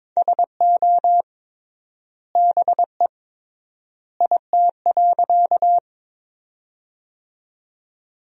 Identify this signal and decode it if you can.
Morse code, heard as a keyed tone, about 23 wpm, 700 Hz. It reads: SO BE IT.
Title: 3x Word-Spacing - Sentences from Top 100 Words - 22wpm